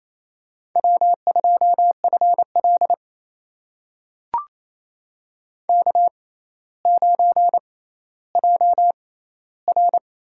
Morse code, heard as a keyed tone, about 28 wpm, spaced as for 14 wpm, 700 Hz. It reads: W2FL K9JR
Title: USA Call Signs - 4 Characters - 6x Character Spacing - 28wpm